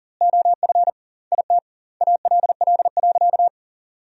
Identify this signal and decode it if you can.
Morse code, heard as a keyed tone, about 37 wpm, 700 Hz. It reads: OF IT ALL.
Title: 2x Word-Spacing - Sentences from Top 200 Words - 40wpm